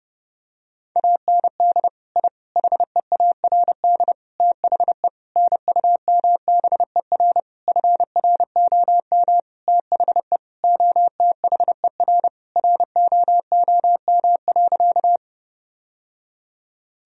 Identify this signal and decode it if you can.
Morse code, heard as a keyed tone, about 32 wpm, 700 Hz. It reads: AND I HEARD THE NUMBER FROM THE OTHER ROOM.
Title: Sentences from Top 1000 Words - Review Part 1 - 30wpm